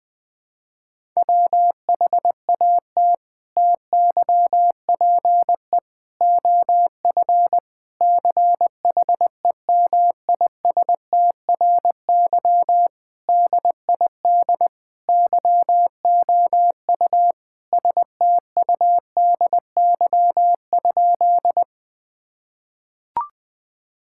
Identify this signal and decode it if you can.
Morse code, heard as a keyed tone, about 21 wpm, 700 Hz. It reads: WHAT TYPE OF CHEMISTRY DID YOU STUDY?
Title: Sentences from Top 1000 Words - Review Part 6 - 20wpm